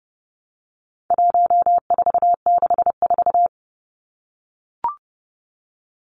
Morse code, heard as a keyed tone, about 30 wpm, 700 Hz. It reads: 1464
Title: Numbers - 4-Digit 30wpm